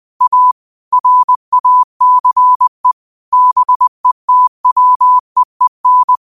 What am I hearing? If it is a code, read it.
Morse code, heard as a keyed tone, about 20 wpm, 1000 Hz. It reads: A RACE BETWEEN
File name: qotdfiles / 20wpm